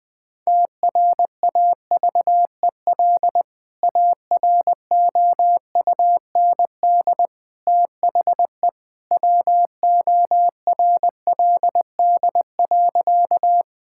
Morse code, heard as a keyed tone, about 20 wpm, 700 Hz. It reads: TRAVEL AROUND THE WORLD.
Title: Sentences with 3-Word Phrases - 20wpm